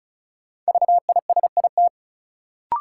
Morse code, heard as a keyed tone, about 36 wpm, 700 Hz. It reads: VISIT
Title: Sentences from Top 900 Words - New Word Drill - 35wpm